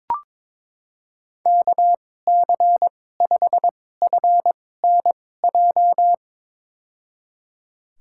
Morse code, heard as a keyed tone, about 22 wpm, spaced as for 16 wpm, 700 Hz. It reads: KC5FNJ
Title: USA Call Signs - 6 Characters - 2x Character Spacing - 22wpm